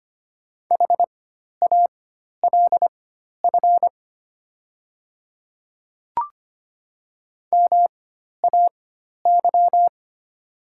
Morse code, heard as a keyed tone, about 25 wpm, spaced as for 12 wpm, 700 Hz. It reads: HALF MAY
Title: 4x Character Spacing - Top 500 Words - 25wpm